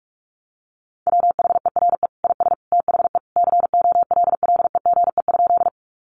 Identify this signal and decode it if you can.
Morse code, heard as a keyed tone, about 44 wpm, 700 Hz. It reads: WHERE IS THE COLLEGE?
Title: Sentences from Top 1300 Words 45wpm